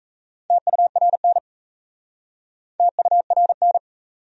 Morse code, heard as a keyed tone, about 38 wpm, 700 Hz. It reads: TURN TURN